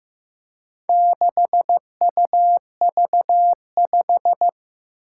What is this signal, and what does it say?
Morse code, heard as a keyed tone, about 15 wpm, 700 Hz. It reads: 6UV5
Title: Four Challenging Letter-Number Combinations 15wpm